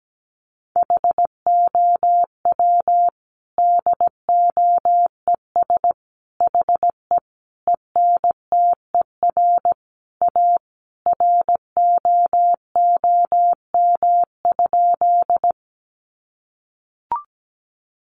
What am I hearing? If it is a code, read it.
Morse code, heard as a keyed tone, about 17 wpm, 700 Hz. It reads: HOW DOES HE ENTER A ROOM?
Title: Sentences from Top 1300 Words 17wpm